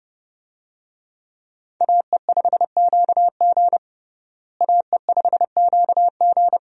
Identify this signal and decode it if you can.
Morse code, heard as a keyed tone, about 30 wpm, 700 Hz. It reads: AE5QG AE5QG
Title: USA Call Signs - 5 Characters - Repeated 3x - 30wpm